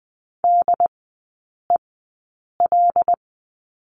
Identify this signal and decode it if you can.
Morse code, heard as a keyed tone, about 20 wpm, spaced as for 8 wpm, 700 Hz. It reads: DEL
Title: 093 Lesson 16e - Letter c - Three Characters 20wpm